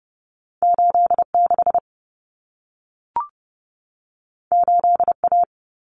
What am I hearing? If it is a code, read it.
Morse code, heard as a keyed tone, about 30 wpm, 700 Hz. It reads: 86 8A
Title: Common Cut Numbers - 2-Digit 30wpm